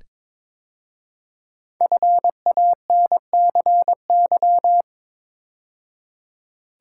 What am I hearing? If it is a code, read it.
Morse code, heard as a keyed tone, about 22 wpm, 700 Hz. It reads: FANCY
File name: 5 Letter Words - First 3 Letters Given - 22wpm